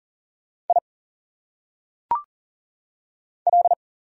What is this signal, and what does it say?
Morse code, heard as a keyed tone, about 40 wpm, 700 Hz. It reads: I L
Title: Single Letters - Phonetic Pronunciation 40wpm